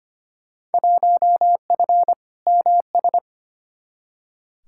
Morse code, heard as a keyed tone, about 25 wpm, 700 Hz. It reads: 1F MS
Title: Field Day Exchanges Encore - 25wpm